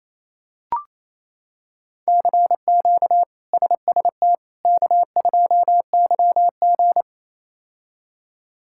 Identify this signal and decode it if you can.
Morse code, heard as a keyed tone, about 28 wpm, 700 Hz. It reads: CQ SST K2YG
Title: K1USN SST QSO Practice - 28wpm